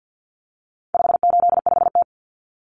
Morse code, heard as a keyed tone, about 50 wpm, 700 Hz. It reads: 585T
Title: Common Cut Numbers - 4-Digit 50wpm